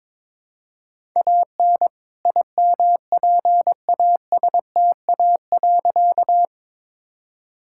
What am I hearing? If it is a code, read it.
Morse code, heard as a keyed tone, about 22 wpm, 700 Hz. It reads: AN IMPASTA.
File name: All-Age Appropriate Jokes - 22wpm